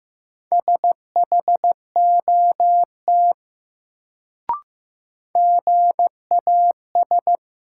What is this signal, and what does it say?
Morse code, heard as a keyed tone, about 16 wpm, 700 Hz. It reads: SHOT GAS
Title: Sentences from Top 1100 Words - New Word Drill Alt - 15wpm